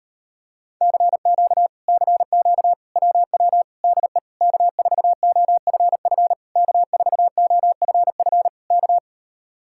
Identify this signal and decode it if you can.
Morse code, heard as a keyed tone, about 38 wpm, 700 Hz. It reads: CQ CQ WW DE K4OFF K4OFF K